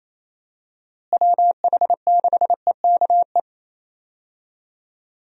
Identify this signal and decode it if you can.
Morse code, heard as a keyed tone, about 27 wpm, 700 Hz. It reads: WH6EKE
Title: USA Call Signs - 6 Characters - Repeated 2x - 28wpm